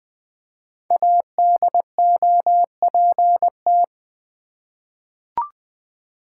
Morse code, heard as a keyed tone, about 20 wpm, 700 Hz. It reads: ADOPT
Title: Five Letter Words - 20wpm